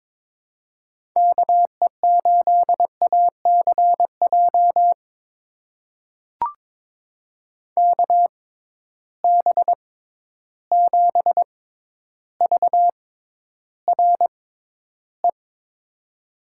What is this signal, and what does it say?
Morse code, heard as a keyed tone, about 22 wpm, spaced as for 11 wpm, 700 Hz. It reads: KE8ACJ KB7VRE
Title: USA Call Signs - 6 Characters - 6x Character Spacing - 22wpm